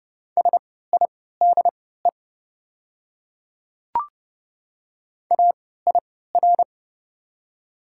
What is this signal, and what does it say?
Morse code, heard as a keyed tone, about 30 wpm, spaced as for 17 wpm, 700 Hz. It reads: SIDE AIR
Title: 3x Character Spacing - Top 300 Words - 30wpm